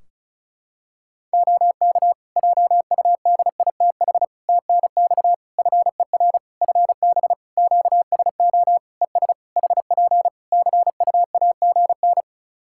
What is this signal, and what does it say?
Morse code, heard as a keyed tone, about 36 wpm, 700 Hz. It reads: OK JUDITH TNX FER FB QSO ES HP CUAGN
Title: Practice QSO - Exchange 4 - Ending - 35wpm